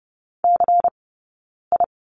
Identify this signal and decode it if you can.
Morse code, heard as a keyed tone, about 30 wpm, spaced as for 9 wpm, 690 Hz. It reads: CI
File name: Lesson 19d - Letter w - Two Characters 30wpm